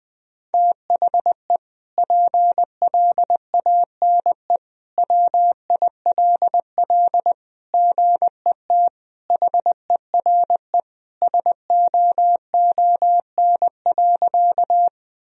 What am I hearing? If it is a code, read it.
Morse code, heard as a keyed tone, about 20 wpm, 700 Hz. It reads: THE PLANE WILL GET HERE SOON.
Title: Sentences from Top 1000 Words - Review Part 3 - 20wpm